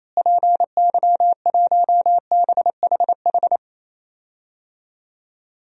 Morse code, heard as a keyed tone, about 27 wpm, 700 Hz. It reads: PY1BHH
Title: DXCC-100 Call Signs - 6 Characters - Repeated 2x - 28wpm